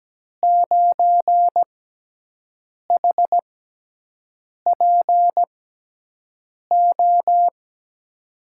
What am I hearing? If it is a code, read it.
Morse code, heard as a keyed tone, about 17 wpm, spaced as for 6 wpm, 700 Hz. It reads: 9HPO
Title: USA Call Signs - 6 Characters - 6x Character Spacing - 17wpm